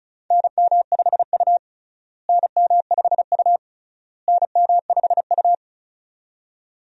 Morse code, heard as a keyed tone, about 36 wpm, 700 Hz. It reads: NM5U NM5U NM5U
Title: USA Call Signs - 4 Characters - Repeated 3x - 35wpm